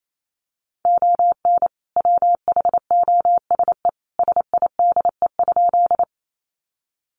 Morse code, heard as a keyed tone, about 28 wpm, 700 Hz. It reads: ON WHOSE SIDE?
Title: Sentences from Top 1200 Words 28wpm